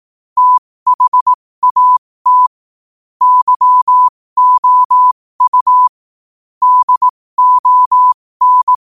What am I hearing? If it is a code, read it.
Morse code, heard as a keyed tone, about 18 wpm, spaced as for 15 wpm, 1000 Hz. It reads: THAT YOU DON
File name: qotdfiles / 15wpm